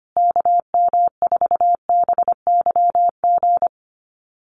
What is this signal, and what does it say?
Morse code, heard as a keyed tone, about 25 wpm, 700 Hz. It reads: KM4BYG
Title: USA Call Signs Repeated 3x 25wpm